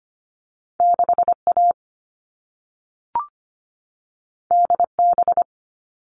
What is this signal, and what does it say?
Morse code, heard as a keyed tone, about 25 wpm, 700 Hz. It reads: 6 A D B
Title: All Cut Numbers - 2-Digit 25wpm